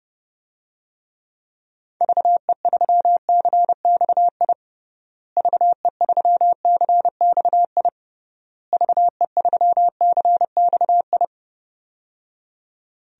Morse code, heard as a keyed tone, about 30 wpm, 700 Hz. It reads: VE3CXI VE3CXI VE3CXI
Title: DXCC-100 Call Signs - 6 Characters - Repeated 3x - 30wpm